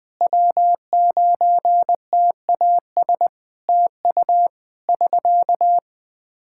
Morse code, heard as a keyed tone, about 20 wpm, 700 Hz. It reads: W9TAS TU <SK>
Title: K1USN SST QSO Practice 20wpm